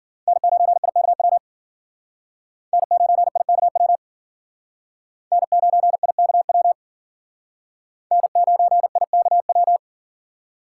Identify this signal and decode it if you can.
Morse code, heard as a keyed tone, about 51 wpm, 700 Hz. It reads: N9IKW N9IKW N9IKW N9IKW